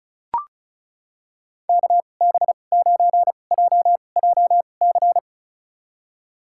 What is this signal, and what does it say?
Morse code, heard as a keyed tone, about 36 wpm, spaced as for 25 wpm, 700 Hz. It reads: KB9JJC
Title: USA Call Signs - 6 Characters - 2x Character Spacing - 35wpm